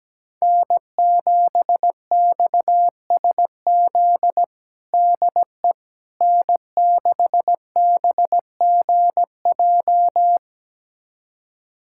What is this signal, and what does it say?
Morse code, heard as a keyed tone, about 17 wpm, 700 Hz. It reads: N7XSZ DE N6BGJ
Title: USA Callsign DE Callsign - 5 Characters - 17wpm